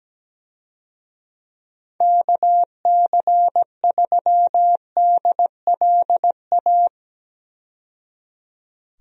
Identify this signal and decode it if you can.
Morse code, heard as a keyed tone, about 17 wpm, 700 Hz. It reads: KC3DLA
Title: USA Call Signs - 6 Characters - 17wpm